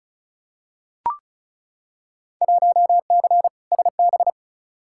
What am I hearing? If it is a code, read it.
Morse code, heard as a keyed tone, about 36 wpm, 690 Hz. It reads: 1C SB
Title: Field Day Exchanges Encore - 35wpm